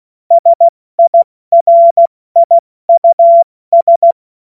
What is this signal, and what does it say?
Morse code, heard as a keyed tone, about 16 wpm, 680 Hz. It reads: SIRIUS